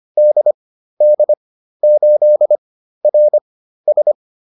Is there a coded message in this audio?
Yes: Morse code from 0.2 to 4.1 s, heard as a keyed tone, about 25 wpm, spaced as for 13 wpm, 600 Hz. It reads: DD8RS